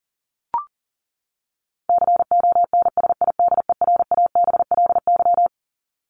Morse code, heard as a keyed tone, about 40 wpm, 700 Hz. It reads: CONSIDERABLY
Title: LY Suffix - Top 100 Words with Base Words 40wpm